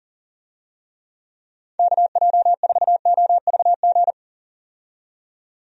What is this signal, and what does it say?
Morse code, heard as a keyed tone, about 38 wpm, 700 Hz. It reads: KJ4OVG
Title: USA Call Signs - 6 Characters - 40wpm